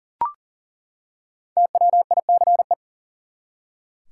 Morse code, heard as a keyed tone, about 40 wpm, 700 Hz. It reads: TWICE
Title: Five Letter Words - 40wpm